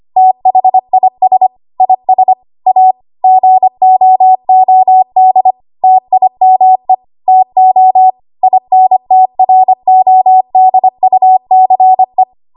Morse code, heard as a keyed tone, about 25 wpm, 750 Hz. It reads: THIS IS A GOOD TIME TO INTRODUCE
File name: Archive / 25 WPM / 25WPM